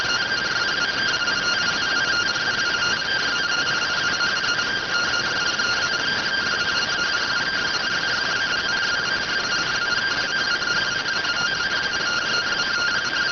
Two-tone teleprinter signal, about 170 Hz shift.